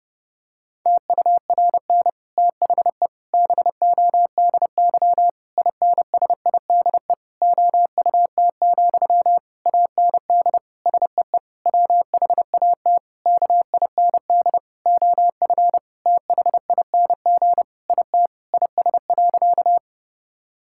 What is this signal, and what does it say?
Morse code, heard as a keyed tone, about 30 wpm, 700 Hz. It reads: TURN THE BODY INSIDE OUT, AND SEE WHAT KIND OF THING IT IS.
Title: Sentences from Top 1000 Words - Review Part 6 - 30wpm